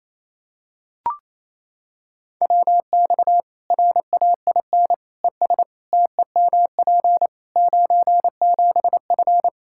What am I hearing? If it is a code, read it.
Morse code, heard as a keyed tone, about 28 wpm, 700 Hz. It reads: WX RAIN ES TEMP 97F
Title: Practice QSO - Exchange 2 - Info - 28wpm